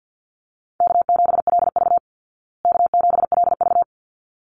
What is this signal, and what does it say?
Morse code, heard as a keyed tone, about 50 wpm, 700 Hz. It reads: K7LV K7LV
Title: Easy Call Signs Repeated 2x - 50wpm